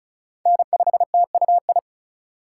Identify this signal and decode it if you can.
Morse code, heard as a keyed tone, about 34 wpm, 700 Hz. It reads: N5TUI
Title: USA Call Signs - 5 Characters - 3x Character Spacing - 35wpm